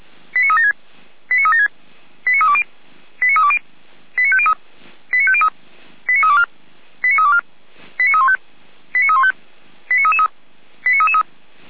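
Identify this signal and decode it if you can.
Five-tone selective call (ZVEI), heard as quick runs of five tones, tones of about 70 ms. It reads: 89376 89376 89320 89320 89503 89503 89324 89324 89315 89315 89333 89333
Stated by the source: ZVEI five-tone call 89109 received by SDR.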